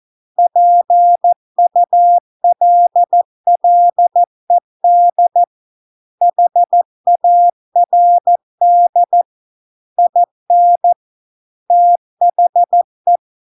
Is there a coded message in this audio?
Yes: Morse code from 0.4 to 13.2 s, heard as a keyed tone, about 14 wpm, 700 Hz. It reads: PULLED HARD IN THE